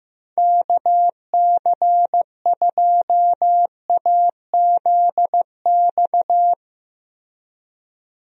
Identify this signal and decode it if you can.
Morse code, heard as a keyed tone, about 16 wpm, 700 Hz. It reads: KC2AZX
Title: USA Call Signs - 6 Characters - Repeated 3x - 15wpm